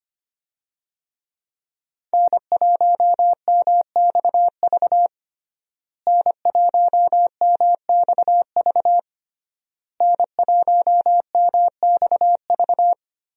Morse code, heard as a keyed tone, about 25 wpm, 700 Hz. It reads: N1MXV N1MXV N1MXV
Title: USA Call Signs - 5 Characters - Repeated 3x - 25wpm